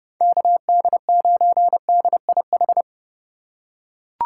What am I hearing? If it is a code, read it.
Morse code, heard as a keyed tone, about 30 wpm, 700 Hz. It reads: KD9DIH